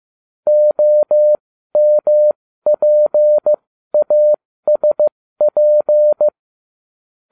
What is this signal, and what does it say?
Morse code, heard as a keyed tone, about 15 wpm, spaced as for 13 wpm, 600 Hz. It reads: OMPASP